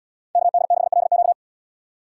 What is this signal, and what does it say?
Morse code, heard as a keyed tone, about 70 wpm, 700 Hz. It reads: KD6LQ